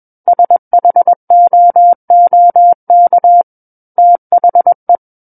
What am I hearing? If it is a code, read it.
Morse code, heard as a keyed tone, about 21 wpm, 700 Hz. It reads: SHOOK THE
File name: Book One / 12-Chapter 12 - 21 WPM